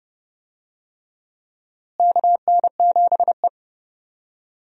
Morse code, heard as a keyed tone, about 30 wpm, 700 Hz. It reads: KN7E
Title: USA Call Signs - 4 Characters - Repeated 3x - 30wpm